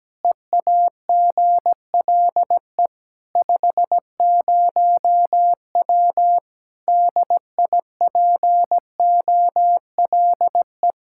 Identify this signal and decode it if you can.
Morse code, heard as a keyed tone, about 17 wpm, 700 Hz. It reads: EAGLE 50W DIPOLE
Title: Practice QSO - Exchange 2 - Info - 17wpm